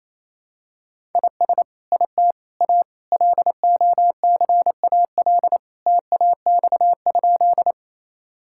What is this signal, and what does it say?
Morse code, heard as a keyed tone, about 28 wpm, 700 Hz. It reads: IS IT A LOCAL TAX?
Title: Sentences from Top 1000 Words - Review Part 5 - 28wpm